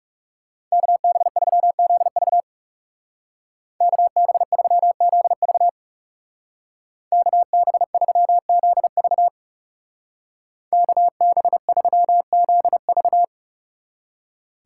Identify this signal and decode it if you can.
Morse code, heard as a keyed tone, about 37 wpm, 700 Hz. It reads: KB3ZV KB3ZV KB3ZV KB3ZV
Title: Speed Racing - USA 2x2 Call Signs 30wpm